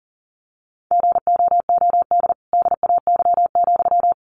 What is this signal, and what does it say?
Morse code, heard as a keyed tone, about 40 wpm, 700 Hz. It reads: GOOD DAY,
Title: Sentences from Top 1400 Words 40wpm